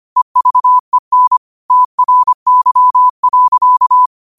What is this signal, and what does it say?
Morse code, heard as a keyed tone, about 25 wpm, 1000 Hz. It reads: EVEN TRY.
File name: 25wpm